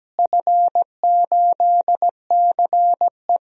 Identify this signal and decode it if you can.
Morse code, heard as a keyed tone, about 17 wpm, 700 Hz. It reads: F8CE
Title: DXCC-100 Call Signs - 4 Characters - Repeated 3x - 17wpm